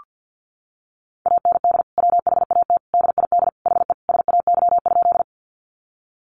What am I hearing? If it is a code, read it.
Morse code, heard as a keyed tone, about 48 wpm, 700 Hz. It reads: AND WHAT DID HE SAY?
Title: Sentences from Top 200 Words 50wpm